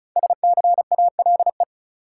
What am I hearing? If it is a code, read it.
Morse code, heard as a keyed tone, about 36 wpm, 700 Hz. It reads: SCALE
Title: Sentences from Top 1000 Words - Review Part 4 - New Word Drill - 35wpm